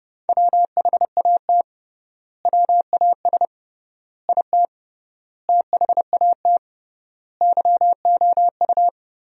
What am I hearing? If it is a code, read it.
Morse code, heard as a keyed tone, about 30 wpm, 700 Hz. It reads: WHAT WAS IT THAT YOU
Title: 3x Word-Spacing - Sentences from Top 300 Words - 30wpm